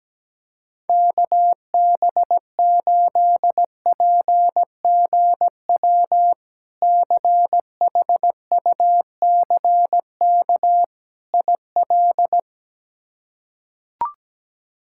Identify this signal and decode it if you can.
Morse code, heard as a keyed tone, about 17 wpm, 700 Hz. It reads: KB8PGW CHUCK IL
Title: K1USN SST QSO Practice - 17wpm